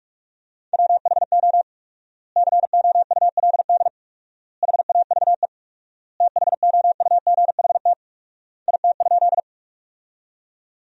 Morse code, heard as a keyed tone, about 43 wpm, 700 Hz. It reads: WHO COULD HAVE THOUGHT IT?
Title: 4x Word-Spacing - Sentences from Top 300 Words - 45wpm